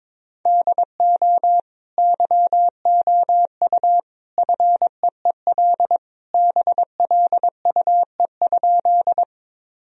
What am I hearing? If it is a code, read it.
Morse code, heard as a keyed tone, about 22 wpm, 700 Hz. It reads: DO YOU FEEL BLUE?
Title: Sentences from Top 1000 Words - Review Part 3 - 22wpm